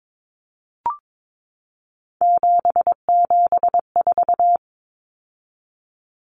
Morse code, heard as a keyed tone, about 22 wpm, 700 Hz. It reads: 774
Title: Common Cut Numbers - 3-Digit 22wpm